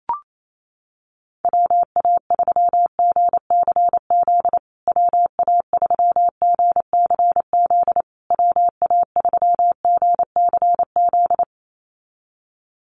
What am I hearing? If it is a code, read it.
Morse code, heard as a keyed tone, about 28 wpm, 700 Hz. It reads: WA3GCZ WA3GCZ WA3GCZ
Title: USA Call Signs Repeated 3x 28wpm